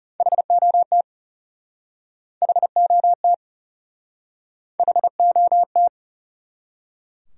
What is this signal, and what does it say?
Morse code, heard as a keyed tone, about 40 wpm, 700 Hz. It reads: HOT HOT HOT